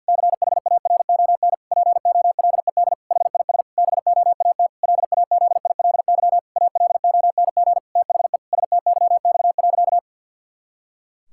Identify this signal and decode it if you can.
Morse code, heard as a keyed tone, about 51 wpm, 700 Hz. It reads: CHARON POLED HIS BOAT LAZILY ALONG THE STYX.
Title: Pangram Challenge v2 - 50wpm